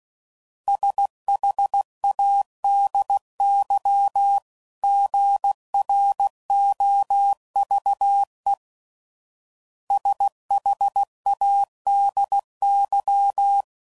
Morse code, heard as a keyed tone, about 16 wpm, 800 Hz. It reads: SHADY GROVE SHADY
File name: CW Code Groups @15WPM / fromSM's / Cities